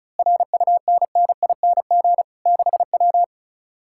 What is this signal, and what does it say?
Morse code, heard as a keyed tone, about 32 wpm, 700 Hz. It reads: RUNNING 6W
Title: Speed-Racing Common QSO Phrases - 35wpm